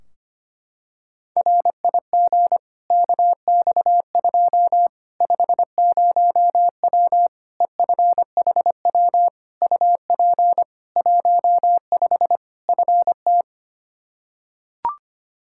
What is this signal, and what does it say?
Morse code, heard as a keyed tone, about 25 wpm, 700 Hz. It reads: RIG KX2 50W EFHW UP 15 FT
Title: K1USN SST QSO Practice 25wpm